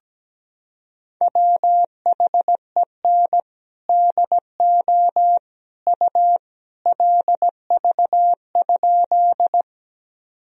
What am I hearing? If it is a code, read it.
Morse code, heard as a keyed tone, about 17 wpm, 700 Hz. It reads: WHEN DO U LV?
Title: Sentences from Top 400 Words and Abbreviations 17wpm